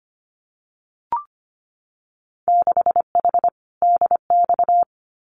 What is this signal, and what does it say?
Morse code, heard as a keyed tone, about 25 wpm, 700 Hz. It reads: 6H DX
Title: Winter Field Day Exchanges Encore 25wpm